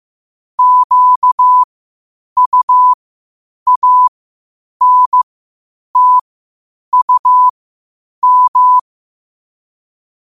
Morse code, heard as a keyed tone, about 15 wpm, spaced as for 8 wpm, 1000 Hz. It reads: QUANTUM
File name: qotdfiles / 8wpm